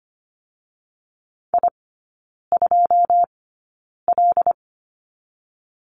Morse code, heard as a keyed tone, about 25 wpm, spaced as for 9 wpm, 700 Hz. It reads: I2L
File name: Lesson 17e - Letter u - Three Characters 25wpm